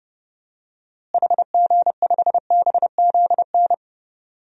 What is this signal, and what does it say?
Morse code, heard as a keyed tone, about 32 wpm, 700 Hz. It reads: HG5BZN
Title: DXCC-100 Call Signs - 6 Characters - Repeated 3x - 30wpm